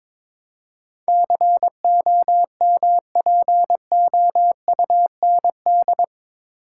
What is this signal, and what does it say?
Morse code, heard as a keyed tone, about 22 wpm, 700 Hz. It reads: COMPOUND
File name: practice / 22wpm / Eight Letter Words - 22wpm